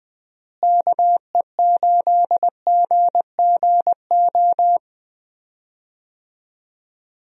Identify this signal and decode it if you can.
Morse code, heard as a keyed tone, about 20 wpm, 700 Hz. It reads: KE8GGO
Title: USA Call Signs - 6 Characters - Repeated 2x - 20wpm